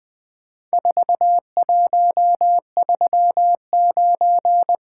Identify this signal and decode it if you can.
Morse code, heard as a keyed tone, about 20 wpm, 700 Hz. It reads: 4139
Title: Four Letter-Number Combinations - 20wpm